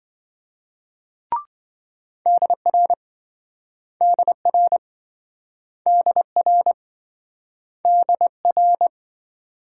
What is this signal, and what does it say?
Morse code, heard as a keyed tone, about 27 wpm, 700 Hz. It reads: DR DR DR DR